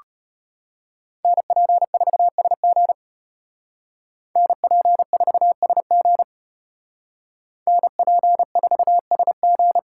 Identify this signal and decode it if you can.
Morse code, heard as a keyed tone, about 32 wpm, 700 Hz. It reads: NP4SG NP4SG NP4SG